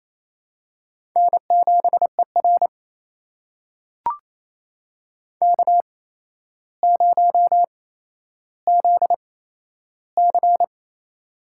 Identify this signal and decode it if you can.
Morse code, heard as a keyed tone, about 28 wpm, spaced as for 12 wpm, 700 Hz. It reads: N7ER K0ZC
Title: USA Call Signs - 4 Characters - 8x Character Spacing - 28wpm